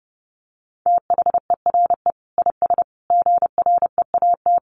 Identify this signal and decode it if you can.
Morse code, heard as a keyed tone, about 30 wpm, 700 Hz. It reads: THERE IS GREAT
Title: Sentences from Top 1200 Words 30wpm